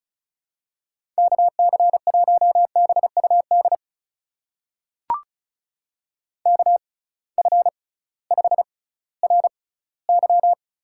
Morse code, heard as a keyed tone, about 36 wpm, spaced as for 19 wpm, 700 Hz. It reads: KC1BUD KF5RY
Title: USA Call Signs - 6 Characters - 6x Character Spacing - 35wpm